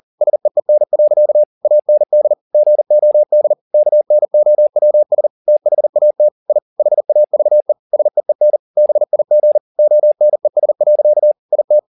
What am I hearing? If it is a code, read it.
Morse code, heard as a keyed tone, about 40 wpm, 600 Hz. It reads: SEEN. AND GOD KNOWS THAT I HAVE SEEN BIG ONES. IT
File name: the old man and the sea 2 40wpm